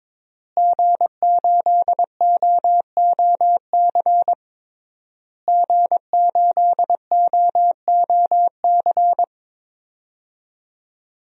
Morse code, heard as a keyed tone, about 22 wpm, 700 Hz. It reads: G8OOC G8OOC
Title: DXCC-100 Call Signs - 5 Characters - Repeated 3x - 22wpm